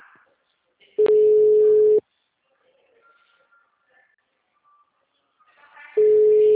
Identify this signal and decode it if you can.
Ringback tone (Europe, 425 Hz).